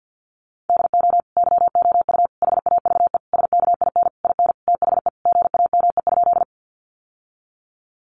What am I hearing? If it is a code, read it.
Morse code, heard as a keyed tone, about 50 wpm, 700 Hz. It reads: DO YOU HAVE SKIN IN THE GAME?
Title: Sentences from Top 1300 Words 50wpm